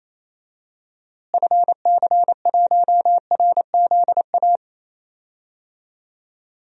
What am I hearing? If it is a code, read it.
Morse code, heard as a keyed tone, about 28 wpm, 700 Hz. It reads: FC1RZA
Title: DXCC-100 Call Signs - 6 Characters - Repeated 2x - 28wpm